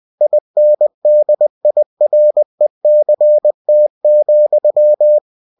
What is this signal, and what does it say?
Morse code, heard as a keyed tone, about 20 wpm, 600 Hz. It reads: INDIRECT,